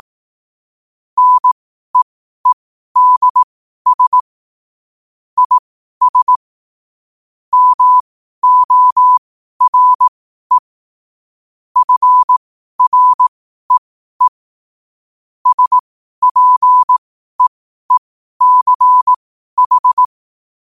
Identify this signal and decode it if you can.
Morse code, heard as a keyed tone, about 18 wpm, spaced as for 12 wpm, 1000 Hz. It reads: NEEDS IS MORE FREE SPEECH
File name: qotdfiles / 12wpm